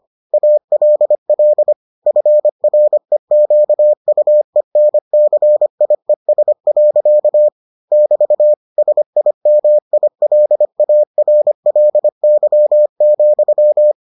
Morse code, heard as a keyed tone, about 25 wpm, 600 Hz. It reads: ALL FREQUENCIES. <BT> SIMILARLY,